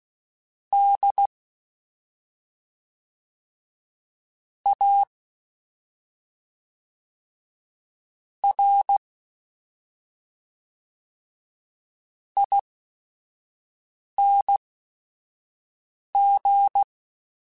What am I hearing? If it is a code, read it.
Morse code, heard as a keyed tone, about 16 wpm, spaced as for 5 wpm, 800 Hz. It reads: D A R ING
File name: Special INGs / Special INGs-mp3s / SlowToFaster ing @5